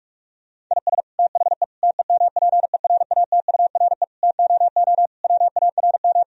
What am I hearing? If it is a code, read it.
Morse code, heard as a keyed tone, about 46 wpm, 700 Hz. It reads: IS THE TEMPERATURE TOO WARM
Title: Sentences from Top 1000 Words - Review Part 4 - 45wpm